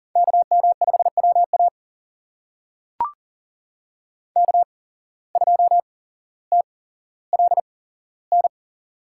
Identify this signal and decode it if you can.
Morse code, heard as a keyed tone, about 40 wpm, spaced as for 17 wpm, 700 Hz. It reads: KM5WA K2TLN
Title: USA Call Signs - 8x Character Spacing - 40wpm